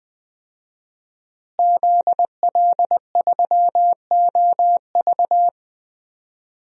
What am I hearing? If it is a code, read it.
Morse code, heard as a keyed tone, about 20 wpm, 700 Hz. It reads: ZL3OV
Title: DXCC-100 Call Signs - 5 Characters - Repeated 3x - 20wpm